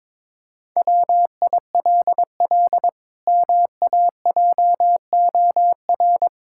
Morse code, heard as a keyed tone, about 22 wpm, 700 Hz. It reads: WILL MAJOR
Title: Pangram Challenge v2 - 22wpm